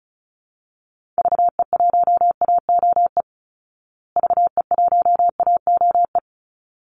Morse code, heard as a keyed tone, about 34 wpm, 700 Hz. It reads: VE1AOE VE1AOE
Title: Call Signs Repeated 3x - 35wpm